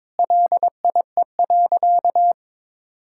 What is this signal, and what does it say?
Morse code, heard as a keyed tone, about 22 wpm, 700 Hz. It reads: LIE.
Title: Sentences from Top 1600 Words 22wpm